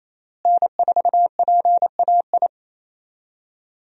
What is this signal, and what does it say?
Morse code, heard as a keyed tone, about 28 wpm, 700 Hz. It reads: N4PAI
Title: USA Call Signs - 5 Characters - Repeated 2x - 28wpm